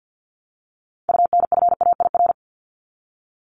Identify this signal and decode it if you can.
Morse code, heard as a keyed tone, about 48 wpm, 700 Hz. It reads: UNFAIR
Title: Top 1751-2000 Words 50wpm